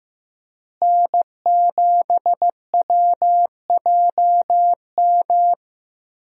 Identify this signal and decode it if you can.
Morse code, heard as a keyed tone, about 15 wpm, 700 Hz. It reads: N7WJM
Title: USA Call Signs - 5 Characters - Repeated 3x - 15wpm